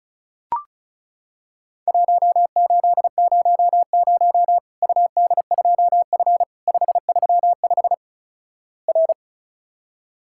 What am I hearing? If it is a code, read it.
Morse code, heard as a keyed tone, about 36 wpm, 700 Hz. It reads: 1800 UD2F 535 R
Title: Worked All Europe DX Contest - QTC Exchange - 35wpm